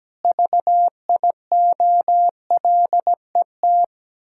Morse code, heard as a keyed tone, about 17 wpm, 700 Hz. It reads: VIOLET
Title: Pangram Challenge v2 - 17wpm